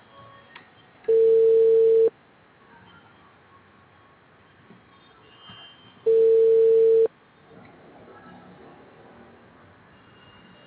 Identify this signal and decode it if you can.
Ringback tone (China, 450 Hz).